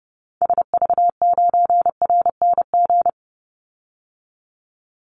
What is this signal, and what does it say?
Morse code, heard as a keyed tone, about 30 wpm, 700 Hz. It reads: SV9RNG
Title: Call Signs Repeated 3x - 30wpm